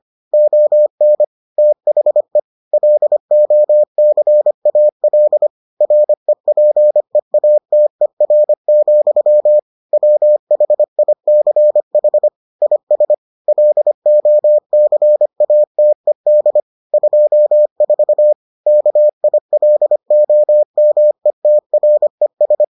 Morse code, heard as a keyed tone, about 25 wpm, 600 Hz. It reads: ON THE LOCAL REPEATER, WHICH IS LOCATED 24 KILOMETRES